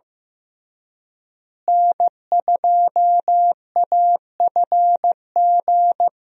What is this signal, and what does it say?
Morse code, heard as a keyed tone, about 15 wpm, 700 Hz. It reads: N2AFG
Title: USA Call Signs - 5 Characters - Repeated 3x - 15wpm